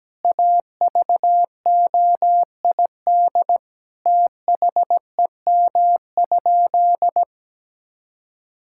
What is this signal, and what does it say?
Morse code, heard as a keyed tone, about 17 wpm, 700 Hz. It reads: AVOID THEM?
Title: Sentences from Top 1000 Words - Review Part 6 - 17wpm